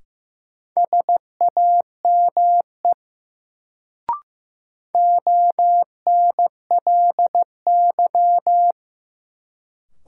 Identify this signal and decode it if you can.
Morse code, heard as a keyed tone, about 15 wpm, 700 Hz. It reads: SAME ONLY
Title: Sentences from Top 200 Words - New Word Drill - 15wpm